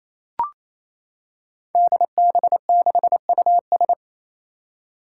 Morse code, heard as a keyed tone, about 28 wpm, 700 Hz. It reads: DB6US
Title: Five Challenging Letter-Number Combinations 28wpm